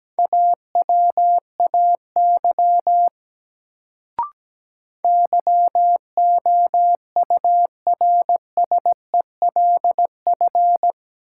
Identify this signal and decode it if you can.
Morse code, heard as a keyed tone, about 17 wpm, 700 Hz. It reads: AWAY YOURSELF
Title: Sentences from Top 600 Words - New Word Drill - 17wpm